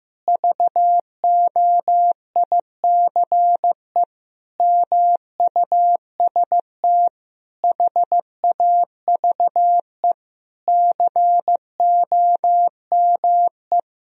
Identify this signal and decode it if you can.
Morse code, heard as a keyed tone, about 15 wpm, 700 Hz. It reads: VOICE MUST HAVE COME
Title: Sentences from Top 1000 Words - Review Part 2 - 15wpm